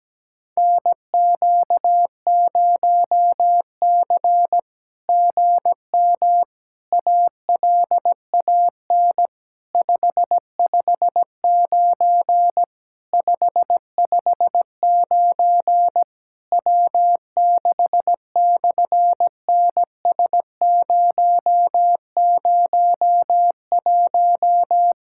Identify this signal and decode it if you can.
Morse code, heard as a keyed tone, about 17 wpm, 700 Hz. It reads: NQ0C GM ALAN 559 559 W6/NS001
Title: SOTA QSO Practice 17wpm